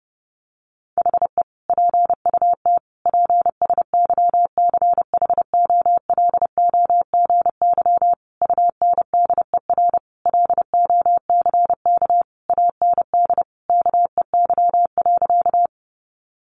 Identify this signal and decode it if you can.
Morse code, heard as a keyed tone, about 30 wpm, 700 Hz. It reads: HE PUT PSYCHOLOGY UNDER LOCK AND KEY.
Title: Sentences from Top 1400 Words 30wpm